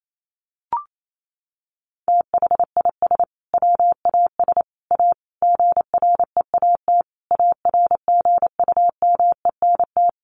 Morse code, heard as a keyed tone, about 28 wpm, 700 Hz. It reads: THIS WAS A GREAT ARGUMENT
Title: Sentences from Top 1400 Words 28wpm